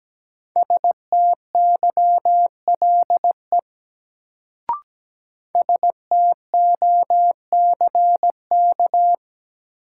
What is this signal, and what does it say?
Morse code, heard as a keyed tone, about 17 wpm, 700 Hz. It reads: STYLE STOCK
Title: Sentences from Top 800 Words - New Word Drill - 17wpm